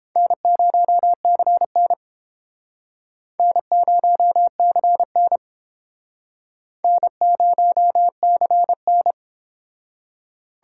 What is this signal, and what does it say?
Morse code, heard as a keyed tone, about 32 wpm, 700 Hz. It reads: N0CN N0CN N0CN